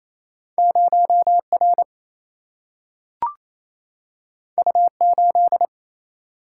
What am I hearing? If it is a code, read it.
Morse code, heard as a keyed tone, about 28 wpm, 700 Hz. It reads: 0R U8
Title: Single-Letter Single-Number 28wpm